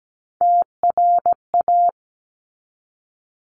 Morse code, heard as a keyed tone, about 17 wpm, 700 Hz. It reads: TRA
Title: Top Three Letter Combinations in Words 17wpm